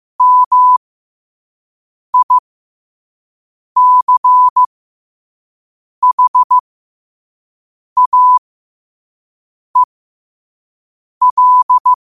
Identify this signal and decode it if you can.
Morse code, heard as a keyed tone, about 15 wpm, spaced as for 5 wpm, 1000 Hz. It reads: MICHAEL
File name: qotdfiles / 5wpm